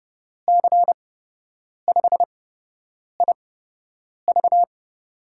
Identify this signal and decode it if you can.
Morse code, heard as a keyed tone, about 30 wpm, spaced as for 8 wpm, 700 Hz. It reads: C5IV
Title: USA Call Signs - 8x Character Spacing - 30wpm